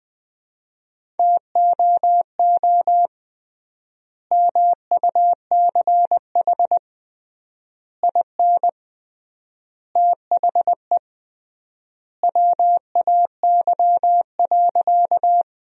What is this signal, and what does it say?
Morse code, heard as a keyed tone, about 20 wpm, 700 Hz. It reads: TOO MUCH IN THE WAY.
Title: 3x Word-Spacing - Sentences from Top 200 Words - 20wpm